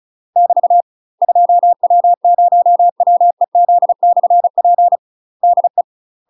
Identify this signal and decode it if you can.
Morse code, heard as a keyed tone, about 34 wpm, 700 Hz. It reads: <BT> 2W0WEZ/P DE
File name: UK MASTER QSOs TRAINING / 35WPM / UK MASTER QSOs 0025